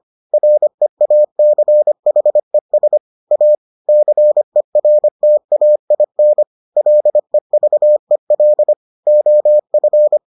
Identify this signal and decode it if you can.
Morse code, heard as a keyed tone, about 25 wpm, 600 Hz. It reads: REACHES A CERTAIN LEVEL OF